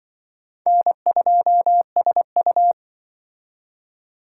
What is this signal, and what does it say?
Morse code, heard as a keyed tone, about 25 wpm, 700 Hz. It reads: N2SU
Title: Speed Racing - USA 1x2 Call Signs 20wpm